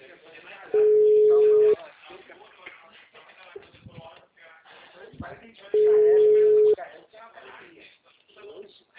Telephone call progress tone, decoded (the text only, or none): ringback tone